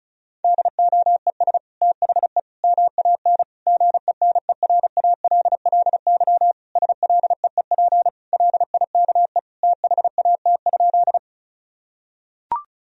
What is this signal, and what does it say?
Morse code, heard as a keyed tone, about 34 wpm, 700 Hz. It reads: DOES THE MAN GENERALLY SLEEP LIKE THAT?
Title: Sentences from Top 1000 Words - Review Part 4 - 35wpm